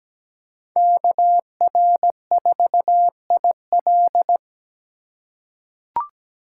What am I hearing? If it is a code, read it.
Morse code, heard as a keyed tone, about 17 wpm, 700 Hz. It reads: KR4IL
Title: USA Call Signs - 5 Characters - 2x Character Spacing - 17wpm